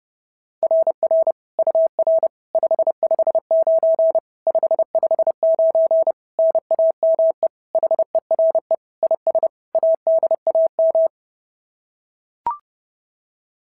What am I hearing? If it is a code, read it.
Morse code, heard as a keyed tone, about 30 wpm, 650 Hz. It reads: RR UR 559 559 NAME HERE IS ADAM
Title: SOTA QSO Practice 30wpm